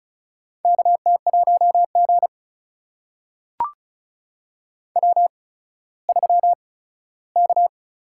Morse code, heard as a keyed tone, about 36 wpm, spaced as for 17 wpm, 700 Hz. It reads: KT1G W3K
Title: USA Call Signs - 4 Characters - 8x Character Spacing - 35wpm